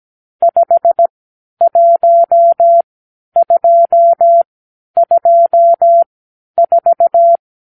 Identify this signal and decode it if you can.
Morse code, heard as a keyed tone, about 17 wpm, spaced as for 10 wpm, 680 Hz. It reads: 51224
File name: arrl / Review (1-5)